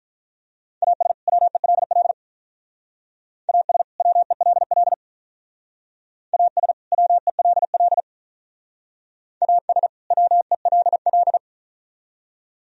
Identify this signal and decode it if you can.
Morse code, heard as a keyed tone, about 44 wpm, 700 Hz. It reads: AS WELL AS WELL AS WELL AS WELL
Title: Speed-Racing - Sets of 2 Words 35wpm